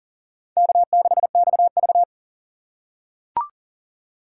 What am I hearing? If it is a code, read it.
Morse code, heard as a keyed tone, about 40 wpm, 700 Hz. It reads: K6XV
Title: USA Call Signs - 4 Characters - 4x Character Spacing - 40wpm